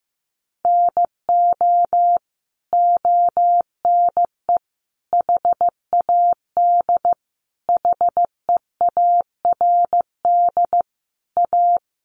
Morse code, heard as a keyed tone, about 15 wpm, 700 Hz. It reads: NO ONE HAD HEARD A
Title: Sentences from Top 1100 Words 15wpm